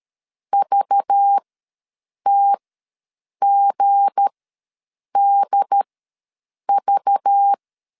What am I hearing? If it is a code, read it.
Morse code, heard as a keyed tone, about 13 wpm, spaced as for 7 wpm, 800 Hz. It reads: VTGDV
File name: LEZIONI / 7 GF9Z / 7 G ws 12wpm 3140